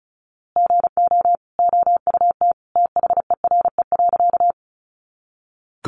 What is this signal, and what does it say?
Morse code, heard as a keyed tone, about 34 wpm, 700 Hz. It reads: GO OUT THERE.
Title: A Sound Of Thunder 35wpm